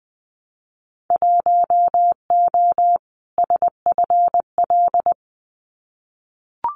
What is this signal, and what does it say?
Morse code, heard as a keyed tone, about 20 wpm, 700 Hz. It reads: 1O SFL
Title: Winter Field Day Exchanges 20wpm